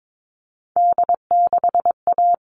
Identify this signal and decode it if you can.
Morse code, heard as a keyed tone, about 22 wpm, 700 Hz. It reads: D6A
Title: All Cut Numbers - 3-Digit 22wpm